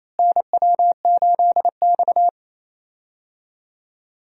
Morse code, heard as a keyed tone, about 28 wpm, 700 Hz. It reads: NW8X
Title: USA Call Signs - 4 Characters - Repeated 3x - 28wpm